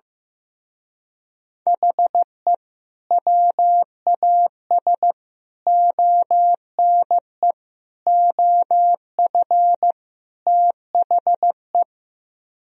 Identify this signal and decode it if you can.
Morse code, heard as a keyed tone, about 16 wpm, 700 Hz. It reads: HE WAS ONE OF THE
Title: Speed-Racing - Sets of 5 Words 15wpm